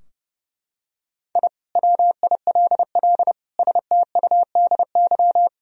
Morse code, heard as a keyed tone, about 32 wpm, 700 Hz. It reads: I WILL STUDY
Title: Sentences from Top 1000 Words - Review Part 2 - 30wpm